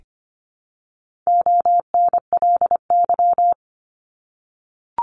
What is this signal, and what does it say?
Morse code, heard as a keyed tone, about 25 wpm, 700 Hz. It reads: ONLY
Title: Top 200 Words 25wpm